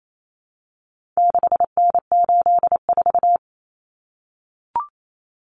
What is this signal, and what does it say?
Morse code, heard as a keyed tone, about 28 wpm, 700 Hz. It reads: 6N84
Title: Common Cut Numbers - 4-Digit 28wpm